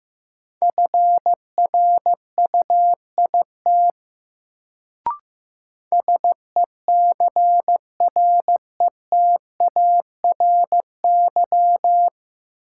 Morse code, heard as a keyed tone, about 15 wpm, 700 Hz. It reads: FRUIT SECRETARY
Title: Sentences from Top 1300 Words - New Word Drill - 15wpm